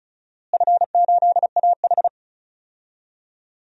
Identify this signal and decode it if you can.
Morse code, heard as a keyed tone, about 36 wpm, 700 Hz. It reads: F8AH
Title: DXCC-100 Call Signs - 4 Characters - Repeated 3x - 35wpm